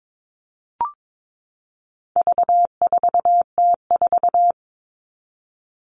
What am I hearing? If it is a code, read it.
Morse code, heard as a keyed tone, about 22 wpm, 700 Hz. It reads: V4T4
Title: All Cut Numbers - 4-Digit 22wpm